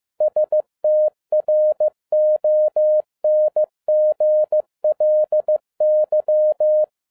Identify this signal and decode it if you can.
Morse code, heard as a keyed tone, about 15 wpm, 610 Hz. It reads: STRONGLY
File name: 2011-07-mp3 / n42 EXCITING QRP CW OPERATION 15wpm